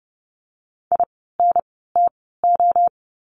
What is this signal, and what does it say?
Morse code, heard as a keyed tone, about 30 wpm, spaced as for 17 wpm, 700 Hz. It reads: INTO
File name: Lesson 17f - Letter u - Words 30wpm